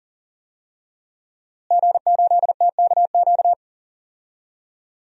Morse code, heard as a keyed tone, about 37 wpm, 700 Hz. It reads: G8TKQ
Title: DXCC-100 Call Signs - 5 Characters - Repeated 2x - 40wpm